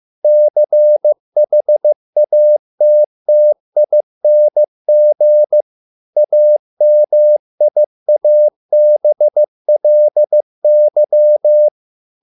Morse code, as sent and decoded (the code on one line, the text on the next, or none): -.-. .... .- - - .. -. --. / .- -- .. .- -... .-.. -.--
CHATTING AMIABLY